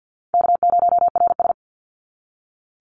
Morse code, heard as a keyed tone, about 52 wpm, 700 Hz. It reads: K0RS
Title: NA SOTA Chaser Callsigns 50wpm